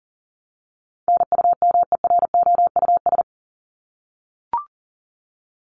Morse code, heard as a keyed tone, about 40 wpm, 700 Hz. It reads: NUMEROUS
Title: Top 1251-1500 Words 40wpm